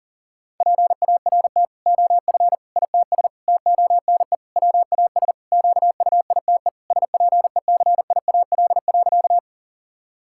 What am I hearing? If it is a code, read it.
Morse code, heard as a keyed tone, about 40 wpm, 700 Hz. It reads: PART OF ITS TONE WAS QUITE SPECIAL.
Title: Sentences from Top 1600 Words 40wpm